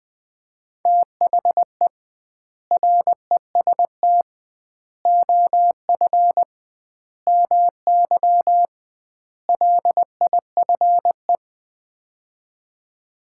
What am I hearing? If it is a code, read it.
Morse code, heard as a keyed tone, about 19 wpm, 700 Hz. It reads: THE REST OF MY LIFE
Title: 2x Word-Spacing - Sets of 5 Words - 20wpm